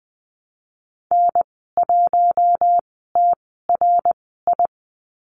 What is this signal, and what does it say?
Morse code, heard as a keyed tone, about 19 wpm, spaced as for 14 wpm, 700 Hz. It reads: N1TRI
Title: 088 Lesson 15g - Number 5 - Call Signs 20wpm